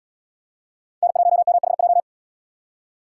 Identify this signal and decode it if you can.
Morse code, heard as a keyed tone, about 74 wpm, 700 Hz. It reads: N1MHJ